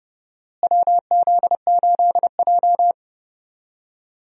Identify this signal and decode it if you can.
Morse code, heard as a keyed tone, about 30 wpm, 700 Hz. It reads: WZ8J
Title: USA Call Signs - 4 Characters - Repeated 2x - 30wpm